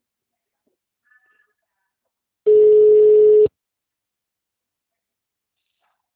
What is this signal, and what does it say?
Ringback tone (Europe, 425 Hz).